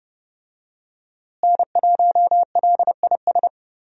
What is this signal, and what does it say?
Morse code, heard as a keyed tone, about 30 wpm, 700 Hz. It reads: N1LIS
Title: USA Call Signs - 5 Characters - Repeated 3x - 30wpm